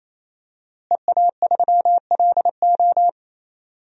Morse code, heard as a keyed tone, about 28 wpm, 700 Hz. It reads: EA3LO
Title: DXCC-100 Call Signs - 5 Characters - Repeated 3x - 28wpm